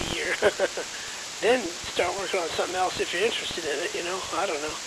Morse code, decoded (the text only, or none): none